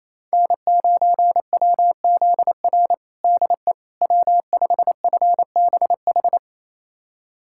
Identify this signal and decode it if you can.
Morse code, heard as a keyed tone, about 28 wpm, 700 Hz. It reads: N9WZR DE W5FBH
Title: USA Callsign DE Callsign - 5 Characters - 28wpm